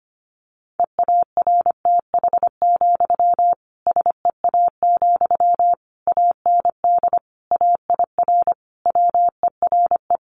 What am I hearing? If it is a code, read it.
Morse code, heard as a keyed tone, about 25 wpm, 700 Hz. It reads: EARTH, SEA, AND AIR WERE
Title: Sentences from Top 1500 Words 25wpm